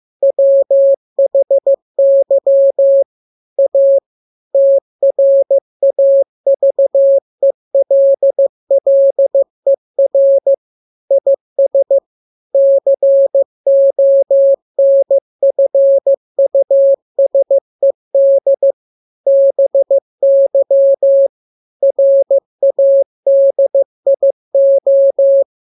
Morse code, heard as a keyed tone, about 15 wpm, 550 Hz. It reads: WHY A TRAVELLER IS CONFUSED BY RADIO
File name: jokes / joke 11 15 0000